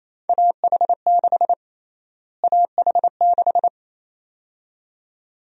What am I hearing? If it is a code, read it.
Morse code, heard as a keyed tone, about 28 wpm, 700 Hz. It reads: AH6 AH6
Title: USA Call Sign Prefixes - Repeated 2x - 28wpm